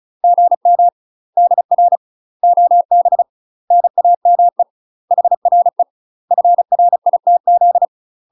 Morse code, heard as a keyed tone, about 36 wpm, 700 Hz. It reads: GM DR OB NAME HRE FRITZ
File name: UK MASTER QSOs TRAINING / 35WPM / UK MASTER QSOs 0016